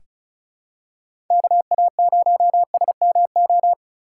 Morse code, heard as a keyed tone, about 34 wpm, 700 Hz. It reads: KA0SMO
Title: USA Call Signs - 6 Characters - 4x Character Spacing - 35wpm